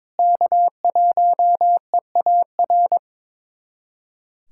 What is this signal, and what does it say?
Morse code, heard as a keyed tone, about 22 wpm, 700 Hz. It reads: K1EAR
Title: USA Callsign DE Callsign 22wpm